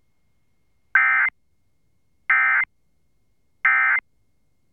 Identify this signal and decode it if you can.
Emergency Alert System (SAME) tones, heard as harsh buzzing bursts. It is heard from 0.9 to 4.0 s.